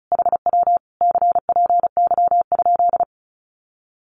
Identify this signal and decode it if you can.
Morse code, heard as a keyed tone, about 36 wpm, 700 Hz. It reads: HW CPY?
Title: Common QSO Phrases Encore 35wpm